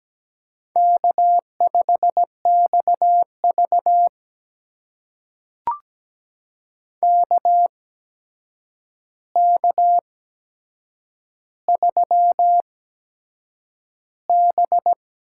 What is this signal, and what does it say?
Morse code, heard as a keyed tone, about 17 wpm, spaced as for 7 wpm, 700 Hz. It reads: K5XV KK3B
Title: USA Call Signs - 4 Characters - 8x Character Spacing - 17wpm